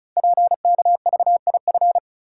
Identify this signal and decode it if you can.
Morse code, heard as a keyed tone, about 36 wpm, 700 Hz. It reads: PKVIF
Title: Code Groups - 5 Letters - 35wpm